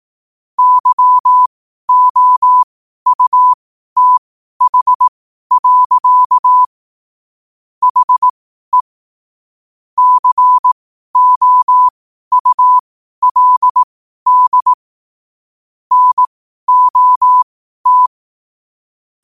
Morse code, heard as a keyed tone, about 18 wpm, spaced as for 12 wpm, 1000 Hz. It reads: YOUTH. HE COULD NOT
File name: qotdfiles / 12wpm